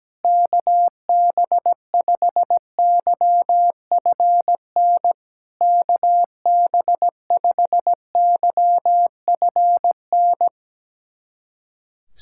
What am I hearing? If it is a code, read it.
Morse code, heard as a keyed tone, about 17 wpm, 700 Hz. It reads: KB5YFN KB5YFN
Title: SOTA QSO Practice 17wpm